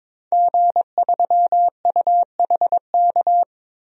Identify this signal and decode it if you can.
Morse code, heard as a keyed tone, about 22 wpm, 700 Hz. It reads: G3UHK
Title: DXCC-100 Call Signs - 5 Characters - 22wpm